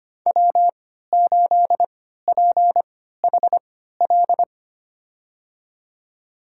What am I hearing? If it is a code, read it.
Morse code, heard as a keyed tone, about 25 wpm, spaced as for 14 wpm, 700 Hz. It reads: W8PHL
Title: USA Call Signs - 5 Characters - 3x Character Spacing - 25wpm